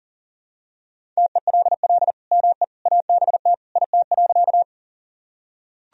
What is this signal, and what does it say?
Morse code, heard as a keyed tone, about 40 wpm, 700 Hz. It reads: TELL ME ABT IT.
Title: Sentences from Top 200 Words and Abbreviations 40wpm